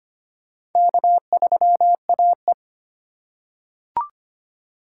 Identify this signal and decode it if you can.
Morse code, heard as a keyed tone, about 25 wpm, 700 Hz. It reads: K3AE
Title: USA Call Signs - 4 Characters - 2x Character Spacing - 25wpm